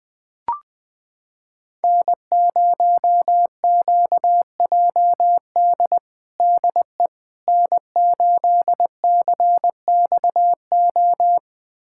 Morse code, heard as a keyed tone, about 21 wpm, 700 Hz. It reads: N0QJD DE N8CXO
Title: USA Callsign DE Callsign - 5 Characters - 20wpm